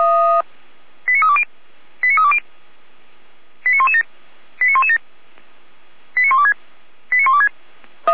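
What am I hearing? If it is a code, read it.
Five-tone selective call (ZVEI), heard as quick runs of five tones, tones of about 70 ms. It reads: 89320 89320 89117 89117 89126 89126
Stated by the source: ZVEI five-tone selective call received by SDR.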